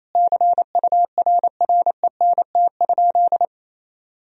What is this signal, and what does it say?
Morse code, heard as a keyed tone, about 28 wpm, 700 Hz. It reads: CURRENT?
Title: Sentences from Top 1000 Words - Review Part 4 - 28wpm